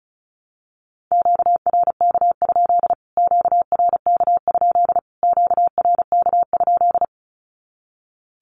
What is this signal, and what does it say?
Morse code, heard as a keyed tone, about 34 wpm, 700 Hz. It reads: QRK? QRK? QRK?
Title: Q-Codes Repeated 3x 35wpm